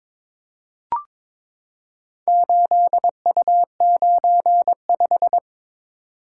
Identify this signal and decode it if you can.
Morse code, heard as a keyed tone, about 22 wpm, 700 Hz. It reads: 8U95
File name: practice / 22wpm / Four Letter-Number Combinations - 22wpm